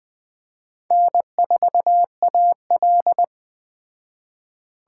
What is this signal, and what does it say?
Morse code, heard as a keyed tone, about 20 wpm, 700 Hz. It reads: N4AL